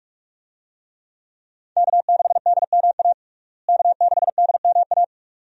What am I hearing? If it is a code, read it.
Morse code, heard as a keyed tone, about 46 wpm, 700 Hz. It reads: K6DMA K6DMA
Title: USA Call Signs - 5 Characters - Repeated 2x - 45wpm